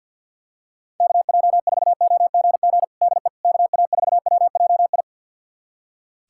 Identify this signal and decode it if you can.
Morse code, heard as a keyed tone, about 46 wpm, 700 Hz. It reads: KJ4OGG DE KA4WJI
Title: USA Callsign DE Callsign - 6 Characters - 50wpm